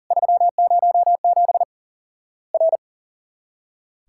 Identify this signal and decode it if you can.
Morse code, heard as a keyed tone, about 40 wpm, 700 Hz. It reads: 307 R
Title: Worked All Europe DX Contest - QTC Exchange - 40wpm